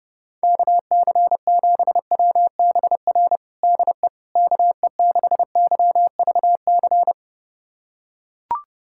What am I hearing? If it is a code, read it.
Morse code, heard as a keyed tone, about 30 wpm, 700 Hz. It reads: KC7WBR DE KE6YVC
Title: USA Callsign DE Callsign - 6 Characters - 30wpm